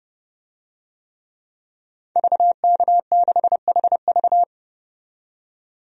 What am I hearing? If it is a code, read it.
Morse code, heard as a keyed tone, about 32 wpm, 700 Hz. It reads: VK6HV
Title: DXCC-100 Call Signs - 5 Characters - 30wpm